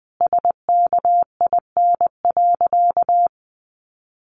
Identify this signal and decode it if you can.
Morse code, heard as a keyed tone, about 20 wpm, 700 Hz. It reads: SKIN.
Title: Sentences from Top 1300 Words 20wpm